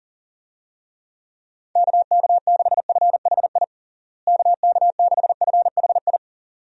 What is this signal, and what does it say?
Morse code, heard as a keyed tone, about 40 wpm, 700 Hz. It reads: KK6FHI KK6FHI
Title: USA Call Signs - 6 Characters - Repeated 2x - 40wpm